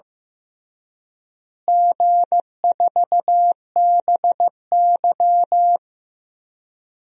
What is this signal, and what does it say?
Morse code, heard as a keyed tone, about 15 wpm, 700 Hz. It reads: G4BY
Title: DXCC-100 Call Signs - 4 Characters - Repeated 3x - 15wpm